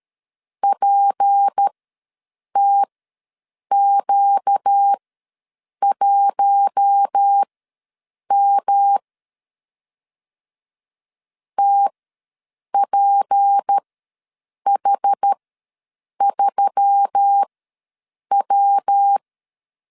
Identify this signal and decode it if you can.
Morse code, heard as a keyed tone, about 13 wpm, spaced as for 7 wpm, 800 Hz. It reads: PTQ1M TPH3W
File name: LEZIONI / 6 PKX0 / 6 P ws 12wpm 3117